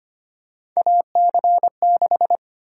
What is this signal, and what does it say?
Morse code, heard as a keyed tone, about 25 wpm, 700 Hz. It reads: AC6
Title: USA Call Sign Prefixes - 25wpm